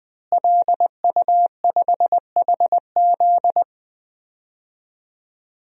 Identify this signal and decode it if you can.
Morse code, heard as a keyed tone, about 20 wpm, 700 Hz. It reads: LU5HZ
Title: DXCC-100 Call Signs - 5 Characters - Repeated 2x - 20wpm